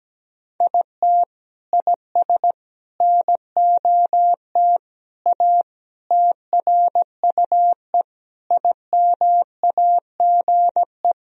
Morse code, heard as a keyed tone, about 17 wpm, 700 Hz. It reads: IT IS NOT A TRUE IMAGE
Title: Sentences from Top 1000 Words - Review Part 4 - 17wpm